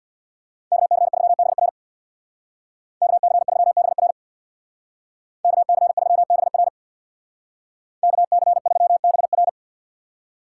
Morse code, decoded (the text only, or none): KC3BR KC3BR KC3BR KC3BR